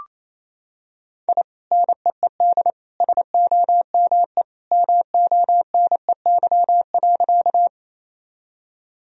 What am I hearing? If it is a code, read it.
Morse code, heard as a keyed tone, about 28 wpm, 700 Hz. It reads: I NEED SOME MONEY.
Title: Sentences from Top 1000 Words - Review Part 1 - 28wpm